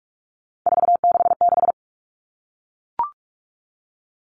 Morse code, heard as a keyed tone, about 44 wpm, 700 Hz. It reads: 466
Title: Numbers - 3-Digit 45wpm